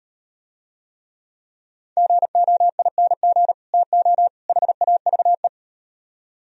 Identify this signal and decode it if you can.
Morse code, heard as a keyed tone, about 38 wpm, 700 Hz. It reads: GOING TO HAVE